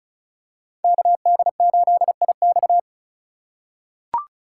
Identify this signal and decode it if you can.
Morse code, heard as a keyed tone, about 36 wpm, 700 Hz. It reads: KD8IX
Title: USA Call Signs - 4x Character Spacing - 35wpm